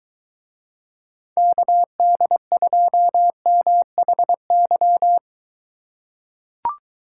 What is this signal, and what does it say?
Morse code, heard as a keyed tone, about 21 wpm, 700 Hz. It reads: KD2MHY